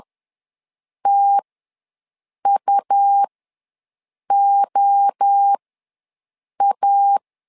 Morse code, heard as a keyed tone, about 10 wpm, spaced as for 6 wpm, 800 Hz. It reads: TUOA
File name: LEZIONI / 3 MWRO / 3 MWRO ws 10wpm 2310